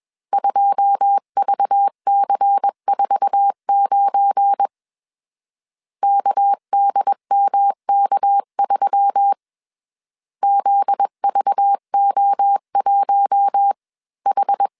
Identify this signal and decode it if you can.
Morse code, heard as a keyed tone, about 21 wpm, 800 Hz. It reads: 2VC49 KDMK3 ZVO1H
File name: CW / DET / 1 9 1500C S C 20wpm 2310